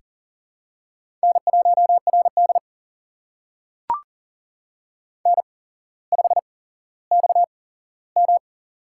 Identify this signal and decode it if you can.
Morse code, heard as a keyed tone, about 40 wpm, spaced as for 17 wpm, 700 Hz. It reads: N1RD N5XM
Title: USA Call Signs - 8x Character Spacing - 40wpm